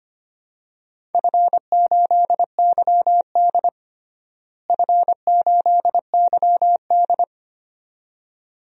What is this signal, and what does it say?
Morse code, heard as a keyed tone, about 25 wpm, 700 Hz. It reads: F8YD F8YD
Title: DXCC-100 Call Signs - 4 Characters - Repeated 2x - 25wpm